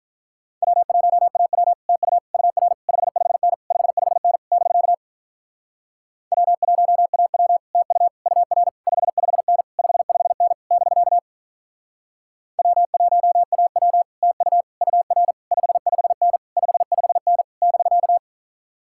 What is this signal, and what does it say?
Morse code, heard as a keyed tone, about 47 wpm, 700 Hz. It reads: W1AW TU UR 55N 55N <BK> W1AW TU UR 55N 55N <BK> W1AW TU UR 55N 55N <BK>